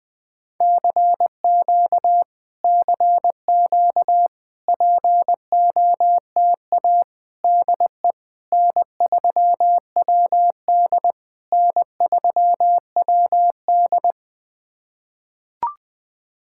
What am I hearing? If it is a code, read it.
Morse code, heard as a keyed tone, about 20 wpm, 660 Hz. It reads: CQ CQ POTA DE N3WD N3WD
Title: POTA QSO Practice - 20wpm